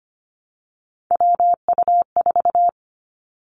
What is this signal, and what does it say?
Morse code, heard as a keyed tone, about 25 wpm, 700 Hz. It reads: WU4
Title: Three Challenging Letter-Number Combinations 25wpm